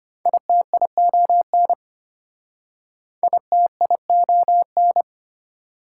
Morse code, heard as a keyed tone, about 34 wpm, 700 Hz. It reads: ITION ITION